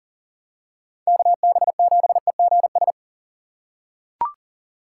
Morse code, heard as a keyed tone, about 40 wpm, 700 Hz. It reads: KB7EGS
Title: USA Call Signs - 4x Character Spacing - 40wpm